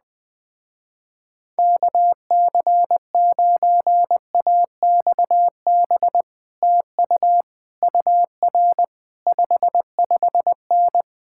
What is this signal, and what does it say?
Morse code, heard as a keyed tone, about 20 wpm, 700 Hz. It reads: KC9AXB TU UR 55N
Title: Speed-Racing Common QSO Phrases - 17wpm